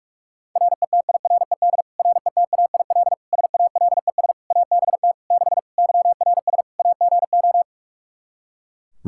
Morse code, heard as a keyed tone, about 46 wpm, 700 Hz. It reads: RETIRED RETAIL SALES ABT 6 YRS AGO
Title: Practice QSO - Exchange 3 - More Info - 45wpm